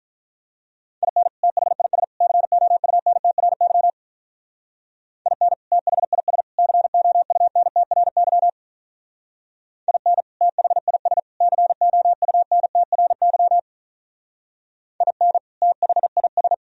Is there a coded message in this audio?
Yes: Morse code from 1.0 to 16.5 s, heard as a keyed tone, about 46 wpm, 700 Hz. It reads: IN THIS COUNTRY IN THIS COUNTRY IN THIS COUNTRY IN THIS